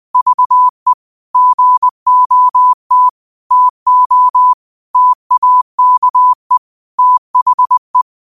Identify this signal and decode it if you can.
Morse code, heard as a keyed tone, about 20 wpm, 1000 Hz. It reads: VE GOT TO TAKE THE
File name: qotdfiles / 20wpm